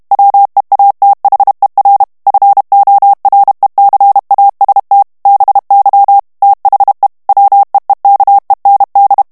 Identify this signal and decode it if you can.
Morse code, heard as a keyed tone, about 32 wpm, 800 Hz. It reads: WEATHER FORECAST BY THE WEEKEND
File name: ANGLAIS texte clair 2 vitesse 30 mots minute